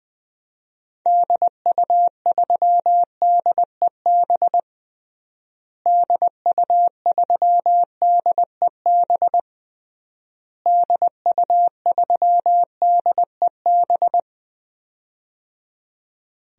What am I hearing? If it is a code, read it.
Morse code, heard as a keyed tone, about 20 wpm, 700 Hz. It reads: DU3DEB DU3DEB DU3DEB
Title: DXCC-100 Call Signs - 6 Characters - Repeated 3x - 20wpm